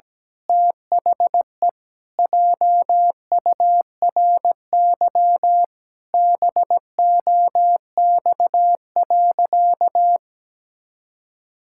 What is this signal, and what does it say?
Morse code, heard as a keyed tone, about 17 wpm, 700 Hz. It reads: THE JURY BOX.
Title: Pangram Challenge v2 - 17wpm